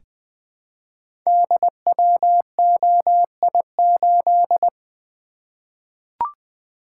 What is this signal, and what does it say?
Morse code, heard as a keyed tone, about 20 wpm, 700 Hz. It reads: DWOI8
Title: Code Groups - 5 Letter:Numbers - 20wpm